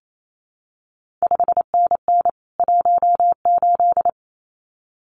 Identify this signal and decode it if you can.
Morse code, heard as a keyed tone, about 28 wpm, 700 Hz. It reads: 5NN 18
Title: CQ WW DX Contest Exchange Encore - 28wpm